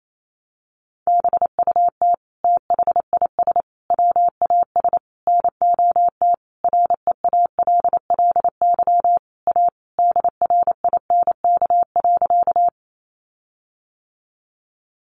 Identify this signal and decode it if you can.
Morse code, heard as a keyed tone, about 28 wpm, 700 Hz. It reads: BUT THIS WAS NOT REALLY A DRINK.
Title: Sentences from Top 1500 Words 28wpm